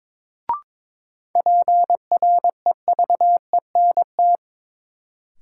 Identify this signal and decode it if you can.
Morse code, heard as a keyed tone, about 22 wpm, 700 Hz. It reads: PREVENT
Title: Sentences from Top 1100 Words - New Word Drill Alt - 22wpm